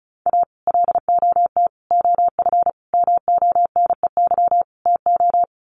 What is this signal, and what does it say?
Morse code, heard as a keyed tone, about 34 wpm, 700 Hz. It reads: A LOT OF MONEY TO
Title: Sets of 5 Words Encore2 35wpm